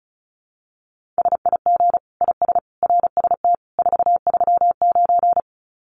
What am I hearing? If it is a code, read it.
Morse code, heard as a keyed tone, about 34 wpm, 700 Hz. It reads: SIG IS RST 439
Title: Common QSO Phrases 35wpm